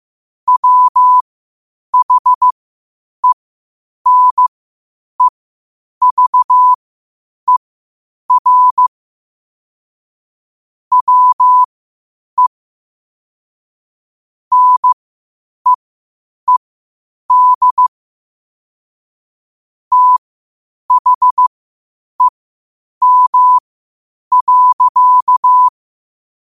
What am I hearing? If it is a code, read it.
Morse code, heard as a keyed tone, about 15 wpm, spaced as for 8 wpm, 1000 Hz. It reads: WHENEVER WE NEED THEM.
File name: qotdfiles / 8wpm